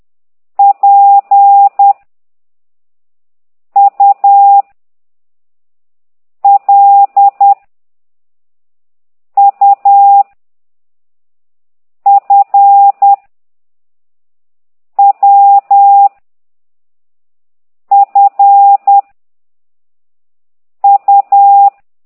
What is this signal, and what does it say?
Morse code, heard as a keyed tone, about 10 wpm, 800 Hz. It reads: P U L U F W F U
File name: ct2idw / exerc9